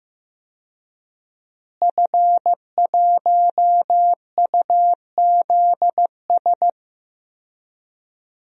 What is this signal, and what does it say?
Morse code, heard as a keyed tone, about 15 wpm, 700 Hz. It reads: F1UZS
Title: DXCC-100 Call Signs - 5 Characters - Repeated 3x - 15wpm